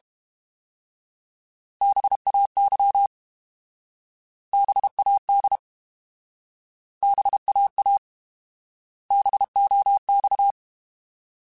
Morse code, heard as a keyed tone, about 32 wpm, 800 Hz. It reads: BAY BAD BAA BOX
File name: Alphabetical / Alphabetical mp3s / Bs @30